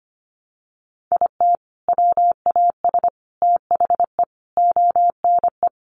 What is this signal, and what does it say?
Morse code, heard as a keyed tone, about 25 wpm, 700 Hz. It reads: IT WAS THE ONE
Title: Sentences from Top 1400 Words 25wpm